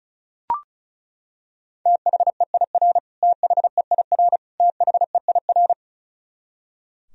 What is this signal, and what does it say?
Morse code, heard as a keyed tone, about 34 wpm, 700 Hz. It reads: THEIR THEIR THEIR
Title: Top 100 Words Repeated 35wpm